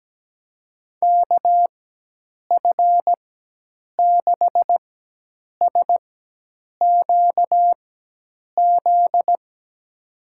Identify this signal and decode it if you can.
Morse code, heard as a keyed tone, about 17 wpm, spaced as for 8 wpm, 700 Hz. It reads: KF6SQZ
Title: USA Call Signs - 6 Characters - 4x Character Spacing - 17wpm